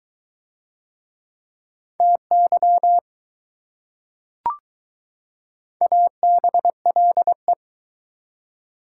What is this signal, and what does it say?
Morse code, heard as a keyed tone, about 20 wpm, 700 Hz. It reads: TY ABLE